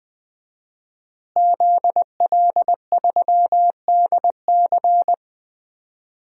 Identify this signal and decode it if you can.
Morse code, heard as a keyed tone, about 20 wpm, 700 Hz. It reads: ZL3DC
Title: DXCC-100 Call Signs - 5 Characters - Repeated 3x - 20wpm